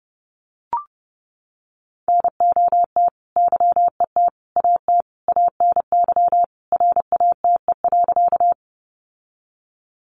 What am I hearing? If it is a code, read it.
Morse code, heard as a keyed tone, about 30 wpm, 700 Hz. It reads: NOT YET AT ANY RATE.
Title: Sentences from Top 700 Words 30wpm